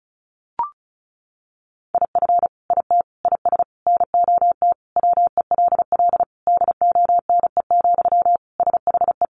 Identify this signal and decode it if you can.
Morse code, heard as a keyed tone, about 34 wpm, 700 Hz. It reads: IF IT IS NOT WELL DONE, SHE
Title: Sentences from Top 1200 Words 35wpm